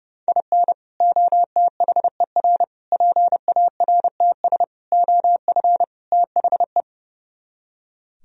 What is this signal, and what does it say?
Morse code, heard as a keyed tone, about 36 wpm, 700 Hz. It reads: IN OTHER PARTS OF THE
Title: Speed-Racing - Sets of 5 Words 30wpm